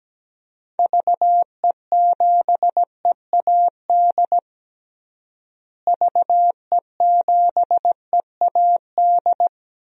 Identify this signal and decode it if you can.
Morse code, heard as a keyed tone, about 17 wpm, 700 Hz. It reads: VE7EAD VE7EAD
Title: DXCC-100 Call Signs - 6 Characters - Repeated 3x - 17wpm